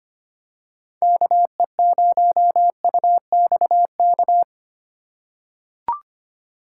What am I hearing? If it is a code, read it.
Morse code, heard as a keyed tone, about 25 wpm, 700 Hz. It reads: KE0UXK
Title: USA Call Signs - 6 Characters - 8x Character Spacing - 25wpm